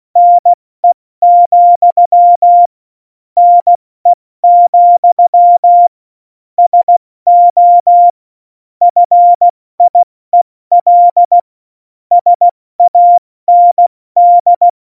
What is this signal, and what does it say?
Morse code, heard as a keyed tone, about 16 wpm, spaced as for 14 wpm, 700 Hz. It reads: NE, NE, SO FIEL SAND